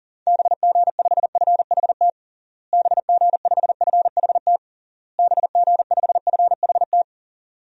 Morse code, heard as a keyed tone, about 40 wpm, 700 Hz. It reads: BG5FHT BG5FHT BG5FHT
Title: DXCC-100 Call Signs - 6 Characters - Repeated 3x - 40wpm